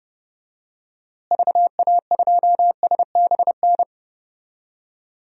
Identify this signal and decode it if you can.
Morse code, heard as a keyed tone, about 31 wpm, 700 Hz. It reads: VA2SBN
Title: DXCC-100 Call Signs - 6 Characters - Repeated 2x - 30wpm